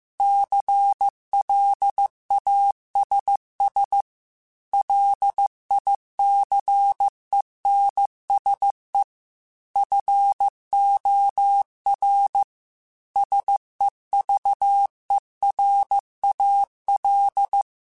Morse code, heard as a keyed tone, about 15 wpm, 800 Hz. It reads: CLASS LICENSE FOR SEVERAL